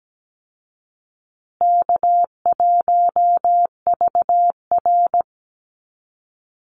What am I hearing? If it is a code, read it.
Morse code, heard as a keyed tone, about 17 wpm, 700 Hz. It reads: K1VR
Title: Easy Call Signs Repeated 3x - 17wpm